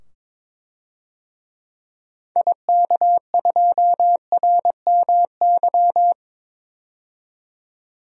Morse code, heard as a keyed tone, about 21 wpm, 700 Hz. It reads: IK2RMY
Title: DXCC-100 Call Signs - 6 Characters - 22wpm